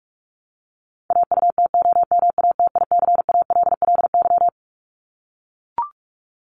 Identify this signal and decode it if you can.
Morse code, heard as a keyed tone, about 44 wpm, 700 Hz. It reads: AUTOMATICALLY
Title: Top 1751-2000 Words 45wpm